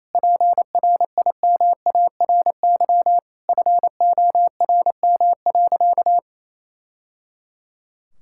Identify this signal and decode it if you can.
Morse code, heard as a keyed tone, about 28 wpm, 700 Hz. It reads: PRIMARY FORM.
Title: Sentences from Top 1000 Words - Review Part 5 - 28wpm